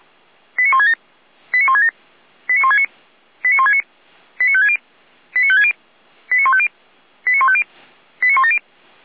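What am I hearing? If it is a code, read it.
Five-tone selective call (ZVEI), heard as quick runs of five tones, tones of about 70 ms. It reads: 89167 89167 89169 89169 89560 89560 89150 89150 89170
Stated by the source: ZVEI five-tone call received by SDR, from the feuerwehr-orlamuende.org (fire brigade Orlamünde) audio server.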